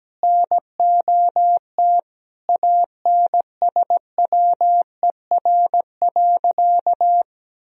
Morse code, heard as a keyed tone, about 17 wpm, 700 Hz. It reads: NOT ANSWER.
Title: Sentences from Top 1000 Words - Review Part 4 - 17wpm